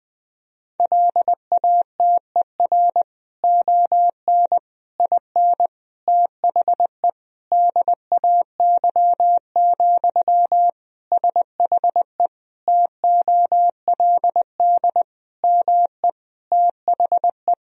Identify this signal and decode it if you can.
Morse code, heard as a keyed tone, about 20 wpm, 700 Hz. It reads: LATER ON IN THE DAY, SHE TOLD ME THE
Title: Sentences from Top 1000 Words - Review Part 4 - 20wpm